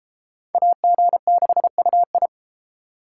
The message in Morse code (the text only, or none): AG6UI